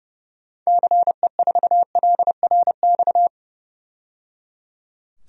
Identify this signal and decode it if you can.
Morse code, heard as a keyed tone, about 29 wpm, 700 Hz. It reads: CE4LRX
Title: DXCC-100 Call Signs - 6 Characters - 30wpm